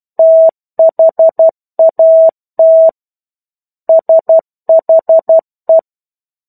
Morse code, heard as a keyed tone, about 12 wpm, 650 Hz. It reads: THAT SHE
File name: 01-Chapter-01 12 WPM